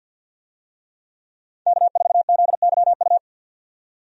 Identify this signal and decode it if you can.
Morse code, heard as a keyed tone, about 50 wpm, 700 Hz. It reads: K4ZYU
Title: USA Call Signs - 5 Characters - 50wpm